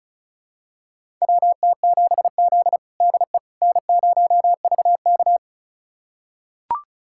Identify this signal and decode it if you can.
Morse code, heard as a keyed tone, about 36 wpm, 700 Hz. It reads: WT7Z DE N0VK
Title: USA Callsign DE Callsign - 4 Characters - 35wpm